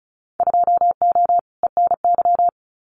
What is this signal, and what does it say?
Morse code, heard as a keyed tone, about 34 wpm, 700 Hz. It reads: 2O ENY
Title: Winter Field Day Exchanges Encore 35wpm